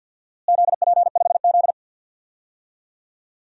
Morse code, heard as a keyed tone, about 52 wpm, 700 Hz. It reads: ZP5Z
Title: DXCC-100 Call Signs - 4 Characters - Repeated 3x - 50wpm